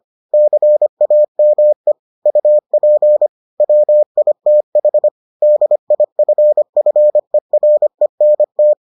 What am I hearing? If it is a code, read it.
Morse code, heard as a keyed tone, about 25 wpm, 600 Hz. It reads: CAME UP WITH DIFFERENT